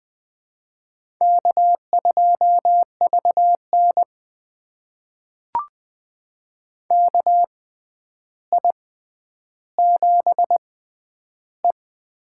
Morse code, heard as a keyed tone, about 21 wpm, spaced as for 10 wpm, 700 Hz. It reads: K2VN KI7E
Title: USA Call Signs - 6x Character Spacing - 20wpm